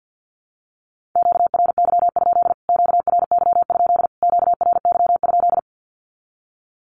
Morse code, heard as a keyed tone, about 48 wpm, 700 Hz. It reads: QRY? QRY? QRY?
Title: Q-Codes Repeated 3x 50wpm